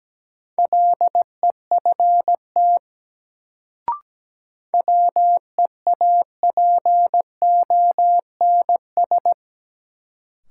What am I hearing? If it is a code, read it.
Morse code, heard as a keyed tone, about 17 wpm, 700 Hz. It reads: LEFT WEAPONS
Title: Sets of 5 Words Encore - New Word Drill - 17wpm